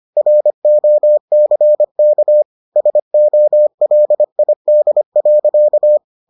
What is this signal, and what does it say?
Morse code, heard as a keyed tone, about 25 wpm, 600 Hz. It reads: ROCK SOLID.